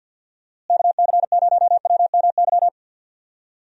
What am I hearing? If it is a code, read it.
Morse code, heard as a keyed tone, about 48 wpm, 700 Hz. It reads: KC0WMY
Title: USA Call Signs - 6 Characters - Repeated 2x - 50wpm